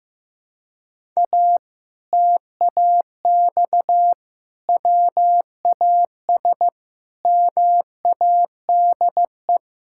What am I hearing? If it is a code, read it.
Morse code, heard as a keyed tone, about 15 wpm, 700 Hz. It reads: A TAX WAS MADE
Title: Sentences from Top 1000 Words - Review Part 6 - 15wpm